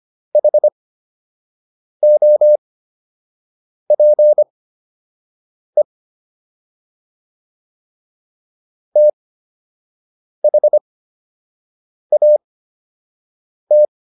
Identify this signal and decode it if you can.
Morse code, heard as a keyed tone, about 25 wpm, spaced as for 6 wpm, 600 Hz. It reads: HOPE THAT